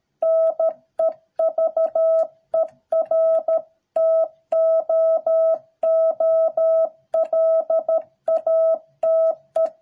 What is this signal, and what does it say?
Morse code, heard as a keyed tone, about 13 wpm, 650 Hz. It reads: NEVER TOO LATE